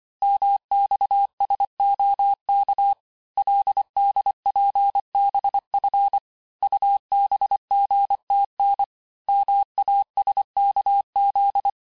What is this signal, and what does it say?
Morse code, heard as a keyed tone, about 25 wpm, 800 Hz. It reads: MXSOK LDPBF UBGTN MASKZ
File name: texte-chiffre-vitesse-23mpm